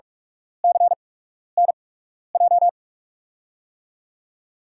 Morse code, heard as a keyed tone, about 42 wpm, spaced as for 12 wpm, 700 Hz. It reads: CNJ
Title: USA Call Signs - 6 Characters - 8x Character Spacing - 45wpm